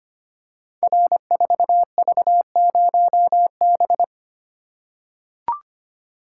Morse code, heard as a keyed tone, about 25 wpm, 700 Hz. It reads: R4V0B
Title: Code Groups - 5 Letter:Numbers - 25wpm